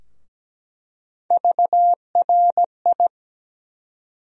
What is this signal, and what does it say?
Morse code, heard as a keyed tone, about 17 wpm, 700 Hz. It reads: VRI
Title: USA Call Sign Postfixes - 3 Letters 17wpm